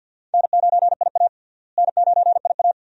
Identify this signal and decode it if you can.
Morse code, heard as a keyed tone, about 52 wpm, 700 Hz. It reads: N9IA N9IA
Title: USA Call Signs - 4 Characters - Repeated 2x - 50wpm